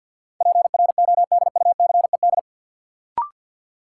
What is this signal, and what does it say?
Morse code, heard as a keyed tone, about 53 wpm, 700 Hz. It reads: PRODUCED
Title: Sentences from Top 1000 Words - Review Part 6 - New Word Drill - 50wpm